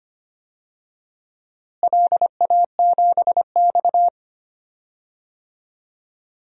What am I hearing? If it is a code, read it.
Morse code, heard as a keyed tone, about 25 wpm, 700 Hz. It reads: LA7X
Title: DXCC-100 Call Signs - 4 Characters - Repeated 2x - 25wpm